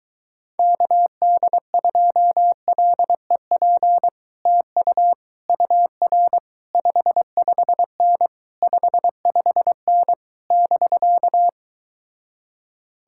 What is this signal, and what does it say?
Morse code, heard as a keyed tone, about 23 wpm, 700 Hz. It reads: KD2LEP TU UR 55N 55N <BK>